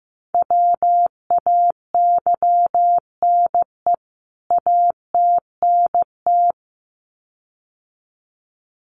Morse code, heard as a keyed tone, about 15 wpm, 700 Hz. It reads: WAYNE ATNT
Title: CWT Exchange 15wpm